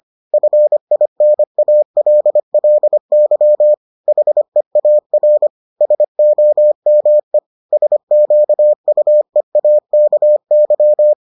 Morse code, heard as a keyed tone, about 25 wpm, 600 Hz. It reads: FINALLY HEAR SOME SQUEAKY